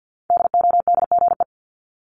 Morse code, heard as a keyed tone, about 50 wpm, 700 Hz. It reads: DODGE
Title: Multi-letter Phonemes Repeated 3x 50wpm